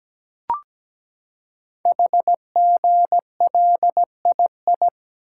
Morse code, heard as a keyed tone, about 17 wpm, 700 Hz. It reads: HGLII
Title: Code Groups - 5 Letters - 17wpm